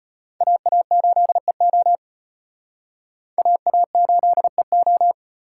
Morse code, heard as a keyed tone, about 31 wpm, 700 Hz. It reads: AA8EO AA8EO